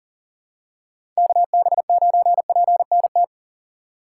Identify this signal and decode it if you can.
Morse code, heard as a keyed tone, about 40 wpm, 700 Hz. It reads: KB9PNT
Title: USA Call Signs - 6 Characters - 4x Character Spacing - 40wpm